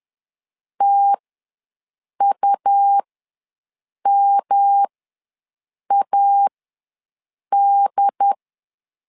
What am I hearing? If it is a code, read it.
Morse code, heard as a keyed tone, about 10 wpm, spaced as for 6 wpm, 800 Hz. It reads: TUMAD
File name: LEZIONI / 3 MWRO / 3 M ws 10wpm 2121